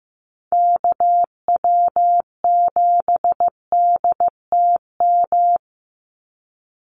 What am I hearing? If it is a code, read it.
Morse code, heard as a keyed tone, about 15 wpm, 700 Hz. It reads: KW7DTM
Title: USA Call Signs 15wpm